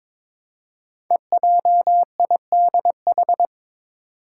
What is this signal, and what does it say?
Morse code, heard as a keyed tone, about 22 wpm, 700 Hz. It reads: EJIDH
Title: Code Groups - 5 Letters - 22wpm